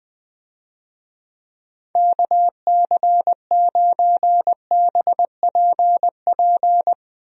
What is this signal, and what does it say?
Morse code, heard as a keyed tone, about 20 wpm, 700 Hz. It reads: KC9BPP
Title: USA Call Signs - 6 Characters - 20wpm